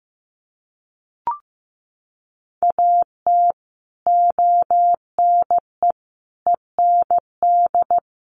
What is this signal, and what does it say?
Morse code, heard as a keyed tone, about 15 wpm, 700 Hz. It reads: AT ONE END
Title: Sets of 5 Words Encore 15wpm